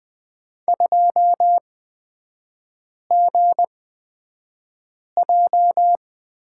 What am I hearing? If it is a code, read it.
Morse code, heard as a keyed tone, about 20 wpm, spaced as for 5 wpm, 700 Hz. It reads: 2GJ
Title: 224 Lesson 36e - BT - Three Characters 20wpm